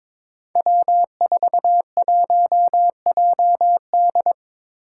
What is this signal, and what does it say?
Morse code, heard as a keyed tone, about 22 wpm, 700 Hz. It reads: W41JD
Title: Five Challenging Letter-Number Combinations 22wpm